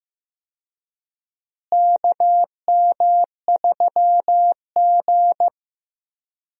Morse code, heard as a keyed tone, about 15 wpm, 700 Hz. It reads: KM3G
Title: USA Call Signs - 4 Characters - Repeated 2x - 15wpm